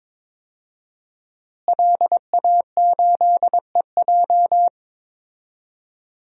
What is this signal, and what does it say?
Morse code, heard as a keyed tone, about 22 wpm, 700 Hz. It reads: LA8EJ
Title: DXCC-100 Call Signs - 5 Characters - Repeated 2x - 22wpm